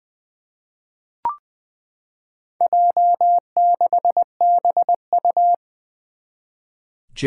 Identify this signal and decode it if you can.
Morse code, heard as a keyed tone, about 20 wpm, 700 Hz. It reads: J6BU
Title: Four Challenging Letter-Number Combinations 20wpm